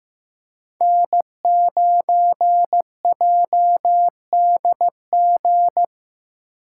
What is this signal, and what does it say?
Morse code, heard as a keyed tone, about 15 wpm, 700 Hz. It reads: N9JDG
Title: USA Call Signs - 3x Character Spacing - 15wpm